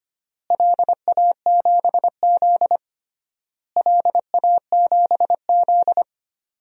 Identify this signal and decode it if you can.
Morse code, heard as a keyed tone, about 25 wpm, 700 Hz. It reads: LA7Z LA7Z
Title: DXCC-100 Call Signs - 4 Characters - Repeated 3x - 25wpm